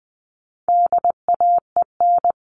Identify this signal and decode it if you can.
Morse code, heard as a keyed tone, about 20 wpm, 700 Hz. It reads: DAEN
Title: All Cut Numbers - 4-Digit 20wpm